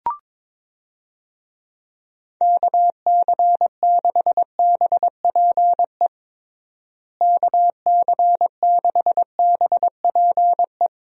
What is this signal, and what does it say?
Morse code, heard as a keyed tone, about 22 wpm, 700 Hz. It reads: KC6BPE KC6BPE
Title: USA Call Signs - 6 Characters - Repeated 2x - 22wpm